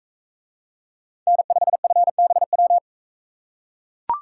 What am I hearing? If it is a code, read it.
Morse code, heard as a keyed tone, about 40 wpm, 700 Hz. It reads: N5FBW